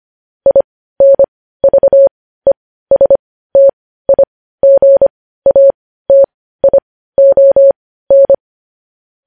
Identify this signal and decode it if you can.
Morse code, heard as a keyed tone, about 25 wpm, spaced as for 15 wpm, 550 Hz. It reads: INVESTIGATION
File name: podcast / News headlines-1x-25x15